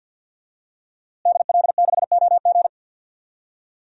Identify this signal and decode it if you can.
Morse code, heard as a keyed tone, about 52 wpm, 700 Hz. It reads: DL6OG
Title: DXCC-100 Call Signs - 5 Characters - Repeated 2x - 50wpm